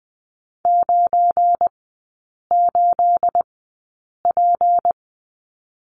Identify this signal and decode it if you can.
Morse code, heard as a keyed tone, about 20 wpm, spaced as for 8 wpm, 700 Hz. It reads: 98P
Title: 212 Lesson 34e - Number 8 - Three Characters 20wpm